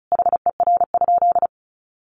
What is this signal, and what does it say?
Morse code, heard as a keyed tone, about 34 wpm, 700 Hz. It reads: HER?
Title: Sentences from Top 400 Words 35wpm